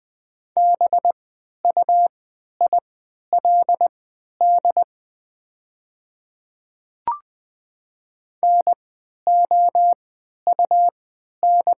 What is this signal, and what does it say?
Morse code, heard as a keyed tone, about 20 wpm, spaced as for 11 wpm, 690 Hz. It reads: BUILD NOUN
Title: 3x Character Spacing - Top 500 Words - 20wpm